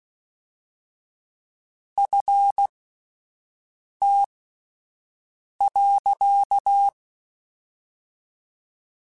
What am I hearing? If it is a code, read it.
Morse code, heard as a keyed tone, about 16 wpm, spaced as for 5 wpm, 800 Hz. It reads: FT.